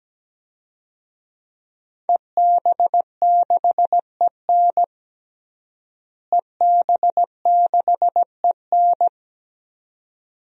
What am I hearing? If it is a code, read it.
Morse code, heard as a keyed tone, about 17 wpm, 700 Hz. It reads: EB6EN EB6EN
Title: DXCC-100 Call Signs - 5 Characters - Repeated 2x - 17wpm